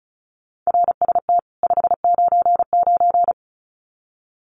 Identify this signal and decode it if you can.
Morse code, heard as a keyed tone, about 34 wpm, 700 Hz. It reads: RST 599
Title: Common QSO Phrases 35wpm